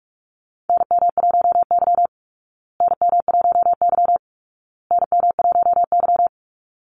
Morse code, heard as a keyed tone, about 44 wpm, 700 Hz. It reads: NM1Y NM1Y NM1Y
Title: Call Signs Repeated 2x - 45wpm